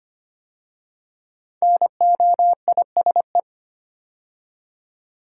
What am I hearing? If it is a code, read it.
Morse code, heard as a keyed tone, about 25 wpm, 700 Hz. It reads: NOISE
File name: 5 Letter Words - First 3 Letters Given - 25wpm